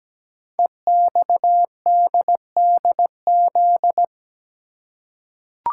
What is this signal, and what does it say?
Morse code, heard as a keyed tone, about 17 wpm, 700 Hz. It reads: EXDDZ
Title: Code Groups - 5 Letters - 17wpm